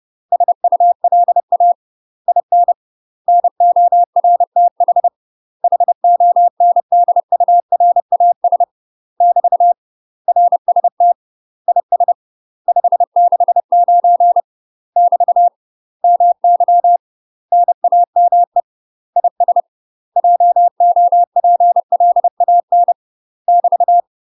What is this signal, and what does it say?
Morse code, heard as a keyed tone, about 30 wpm, 700 Hz. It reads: SULA IN NORTH HONDURAS <BT> RST IS 569 <BT> MY NAME IS JOPLAN <BT>
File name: UK MASTER QSOs TRAINING / 30WPM / UK MASTER QSOs 0007